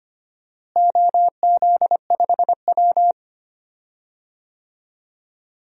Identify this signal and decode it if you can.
Morse code, heard as a keyed tone, about 25 wpm, 700 Hz. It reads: OZ5W
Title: DXCC-100 Call Signs - 4 Characters - Repeated 3x - 25wpm